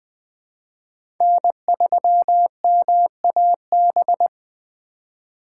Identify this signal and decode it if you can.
Morse code, heard as a keyed tone, about 20 wpm, 700 Hz. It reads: N3MAB
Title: USA Call Signs - 5 Characters - 3x Character Spacing - 20wpm